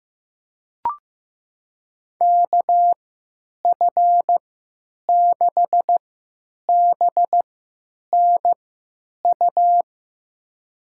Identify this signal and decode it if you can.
Morse code, heard as a keyed tone, about 15 wpm, spaced as for 9 wpm, 700 Hz. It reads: KF6BNU
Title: USA Call Signs - 6 Characters - 3x Character Spacing - 15wpm